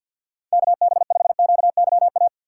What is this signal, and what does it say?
Morse code, heard as a keyed tone, about 53 wpm, 700 Hz. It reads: KB5QYA
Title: USA Call Signs - 6 Characters - 8x Character Spacing - 50wpm